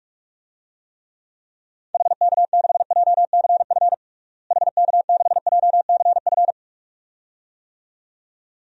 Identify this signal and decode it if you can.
Morse code, heard as a keyed tone, about 44 wpm, 700 Hz. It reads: HK6JCF HK6JCF
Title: DXCC-100 Call Signs - 6 Characters - Repeated 2x - 45wpm